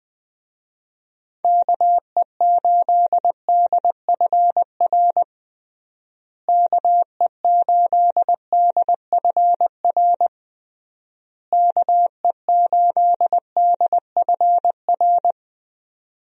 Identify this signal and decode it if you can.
Morse code, heard as a keyed tone, about 21 wpm, 700 Hz. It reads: KE8DFR KE8DFR KE8DFR
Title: USA Call Signs - 6 Characters - Repeated 3x - 20wpm